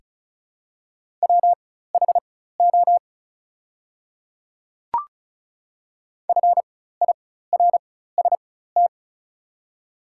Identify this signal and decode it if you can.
Morse code, heard as a keyed tone, about 36 wpm, spaced as for 16 wpm, 700 Hz. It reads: WHO FIRST
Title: 4x Character Spacing - Top 300 Words - 35wpm